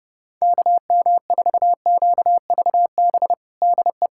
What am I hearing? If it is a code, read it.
Morse code, heard as a keyed tone, about 30 wpm, 700 Hz. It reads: KM4QVB DE
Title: USA Callsign DE Callsign - 6 Characters - 30wpm